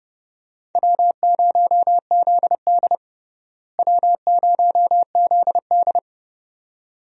Morse code, heard as a keyed tone, about 29 wpm, 700 Hz. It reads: W0ZD W0ZD
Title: USA Call Signs - 4 Characters - Repeated 2x - 30wpm